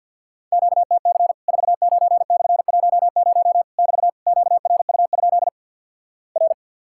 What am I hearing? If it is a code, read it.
Morse code, heard as a keyed tone, about 50 wpm, 700 Hz. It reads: QTC 49/10 <BT> QRU? R
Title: Worked All Europe DX Contest - QTC Exchange with Repeats - 50wpm